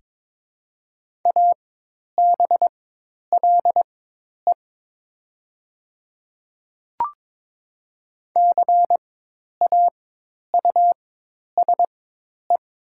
Morse code, heard as a keyed tone, about 22 wpm, spaced as for 10 wpm, 700 Hz. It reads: ABLE CAUSE
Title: 4x Character Spacing - Top 500 Words - 22wpm